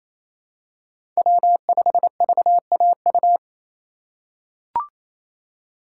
Morse code, heard as a keyed tone, about 28 wpm, 700 Hz. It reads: W5VAU
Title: USA Call Signs - 5 Characters - 2x Character Spacing - 28wpm